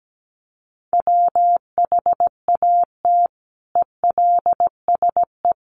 Morse code, heard as a keyed tone, about 17 wpm, 700 Hz. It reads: WHAT ELSE
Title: Sentences from Top 800 Words 17wpm